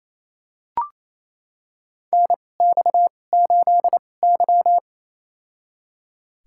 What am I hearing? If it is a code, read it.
Morse code, heard as a keyed tone, about 28 wpm, spaced as for 20 wpm, 700 Hz. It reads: NX8Y
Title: USA Call Signs - 4 Characters - 2x Character Spacing - 28wpm